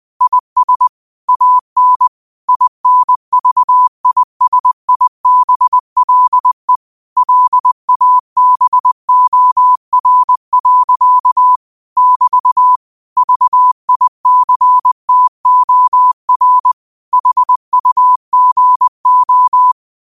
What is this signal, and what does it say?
Morse code, heard as a keyed tone, about 20 wpm, 1000 Hz. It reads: IS AN INVISIBLE LABOR. <BT> VICTOR HUGO
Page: New quotes every day in morse code at 20 Words per minute.